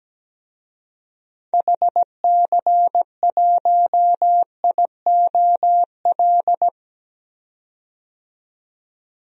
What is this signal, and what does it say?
Morse code, heard as a keyed tone, about 17 wpm, 700 Hz. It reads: HC1IOL
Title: DXCC-100 Call Signs - 6 Characters - 17wpm